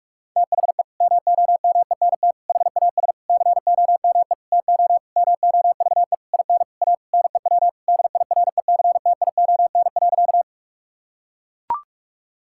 Morse code, heard as a keyed tone, about 41 wpm, 700 Hz. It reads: THE MOMENT HAS COME TO MOVE IN A NEW DIRECTION.
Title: Sentences from Top 1000 Words - Review Part 5 - 45wpm